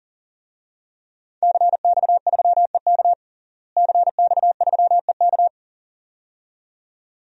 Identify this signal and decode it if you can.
Morse code, heard as a keyed tone, about 40 wpm, 700 Hz. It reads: CX3EK CX3EK
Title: DXCC-100 Call Signs - 5 Characters - Repeated 2x - 40wpm